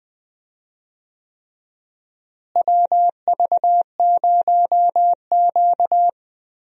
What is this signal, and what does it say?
Morse code, heard as a keyed tone, about 20 wpm, 700 Hz. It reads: WV0Q
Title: USA Call Signs - 4 Characters - Repeated 3x - 20wpm